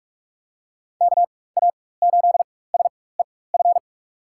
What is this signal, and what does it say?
Morse code, heard as a keyed tone, about 44 wpm, spaced as for 21 wpm, 700 Hz. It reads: KA8SEF
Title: USA Call Signs - 6 Characters - 4x Character Spacing - 45wpm